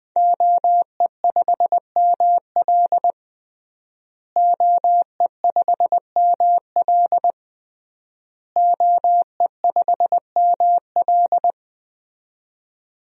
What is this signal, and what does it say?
Morse code, heard as a keyed tone, about 20 wpm, 700 Hz. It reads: OE5ML OE5ML OE5ML
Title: DXCC-100 Call Signs - 5 Characters - Repeated 3x - 20wpm